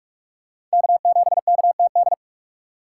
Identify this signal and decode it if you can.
Morse code, heard as a keyed tone, about 44 wpm, 700 Hz. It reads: K7KTD